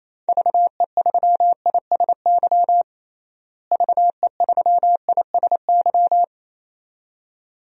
Morse code, heard as a keyed tone, about 28 wpm, 700 Hz. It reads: VE3ISY VE3ISY
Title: DXCC-100 Call Signs - 6 Characters - Repeated 2x - 28wpm